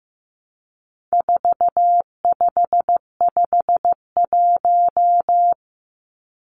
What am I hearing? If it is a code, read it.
Morse code, heard as a keyed tone, about 15 wpm, 700 Hz. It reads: 4551
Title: Numbers - 4-Digit 15wpm